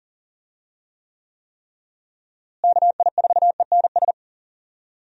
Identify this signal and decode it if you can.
Morse code, heard as a keyed tone, about 40 wpm, 700 Hz. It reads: KI4ENS
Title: USA Call Signs - 6 Characters - Repeated 3x - 40wpm